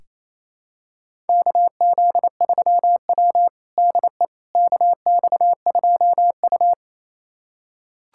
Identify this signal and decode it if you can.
Morse code, heard as a keyed tone, about 27 wpm, 700 Hz. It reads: KZ3W DE KX2U
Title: USA Callsign DE Callsign - 4 Characters - 28wpm